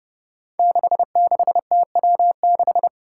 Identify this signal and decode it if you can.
Morse code, heard as a keyed tone, about 30 wpm, 700 Hz. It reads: 66TW6
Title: Five Challenging Letter-Number Combinations 30wpm